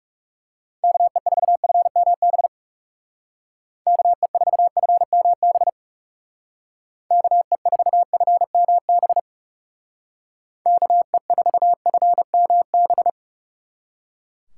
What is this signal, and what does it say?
Morse code, heard as a keyed tone, about 36 wpm, 700 Hz. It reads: KE4FMB KE4FMB KE4FMB KE4FMB
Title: Speed Racing - USA 2x3 Call Signs 30wpm